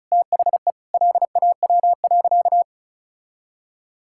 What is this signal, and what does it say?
Morse code, heard as a keyed tone, about 36 wpm, 700 Hz. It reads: THE LAW.
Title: Sentences from Top 1000 Words - Review Part 5 - 35wpm